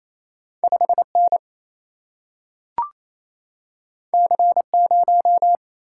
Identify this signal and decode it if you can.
Morse code, heard as a keyed tone, about 28 wpm, 700 Hz. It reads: 5N C0
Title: Single-Letter Single-Number 28wpm